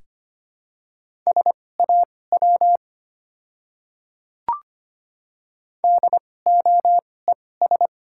Morse code, heard as a keyed tone, about 25 wpm, spaced as for 18 wpm, 690 Hz. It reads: SAW DOES
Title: 2x Character Spacing - Top 300 Words - 25wpm